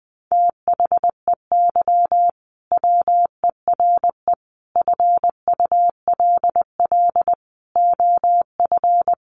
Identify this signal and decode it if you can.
Morse code, heard as a keyed tone, about 20 wpm, 700 Hz. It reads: THEY WERE FULL OF
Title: Sentences from Top 1400 Words 20wpm